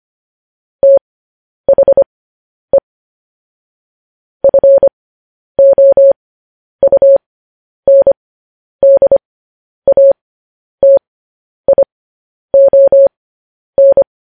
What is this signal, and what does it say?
Morse code, heard as a keyed tone, about 25 wpm, spaced as for 10 wpm, 550 Hz. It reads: THE FOUNDATION
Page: news headlines in morse code at 10 wpm